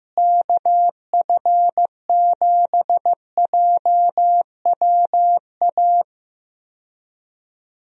Morse code, heard as a keyed tone, about 16 wpm, 700 Hz. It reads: KF7JWA
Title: USA Call Signs - 6 Characters - Repeated 2x - 15wpm